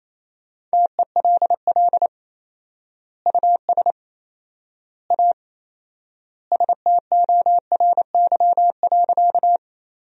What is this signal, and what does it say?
Morse code, heard as a keyed tone, about 27 wpm, 700 Hz. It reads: TELL US A STORY.
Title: 4x Word-Spacing - Sentences from Top 300 Words - 28wpm